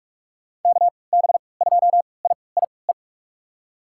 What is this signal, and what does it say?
Morse code, heard as a keyed tone, about 45 wpm, spaced as for 25 wpm, 700 Hz. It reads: KB2IIE
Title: USA Call Signs - 6 Characters - 3x Character Spacing - 45wpm